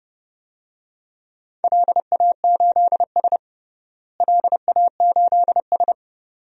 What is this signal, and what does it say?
Morse code, heard as a keyed tone, about 32 wpm, 700 Hz. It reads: LA8S LA8S
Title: DXCC-100 Call Signs - 4 Characters - Repeated 2x - 30wpm